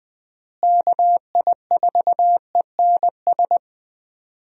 Morse code, heard as a keyed tone, about 21 wpm, 700 Hz. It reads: KI4ENS
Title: USA Call Signs - 6 Characters - Repeated 3x - 20wpm